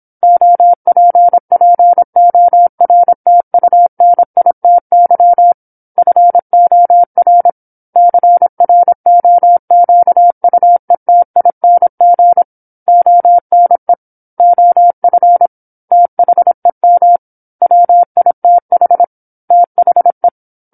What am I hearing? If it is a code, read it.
Morse code, heard as a keyed tone, about 27 wpm, 700 Hz. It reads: OPPORTUNITY FOR CROQUETING ONE OF THEM WITH THE
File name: 08-Chapter-08 26 WPM